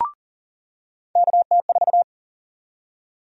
Morse code, heard as a keyed tone, about 39 wpm, 700 Hz. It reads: KT4
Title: USA Call Sign Prefixes - 40wpm